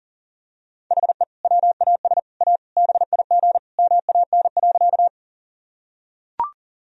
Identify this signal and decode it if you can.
Morse code, heard as a keyed tone, about 40 wpm, 700 Hz. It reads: HE WAS A BIG MAN.
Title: Sentences from Top 1000 Words - Review Part 1 - 40wpm